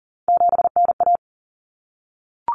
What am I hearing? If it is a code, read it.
Morse code, heard as a keyed tone, about 40 wpm, 700 Hz. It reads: 7NA
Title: Common Cut Numbers - 3-Digit 40wpm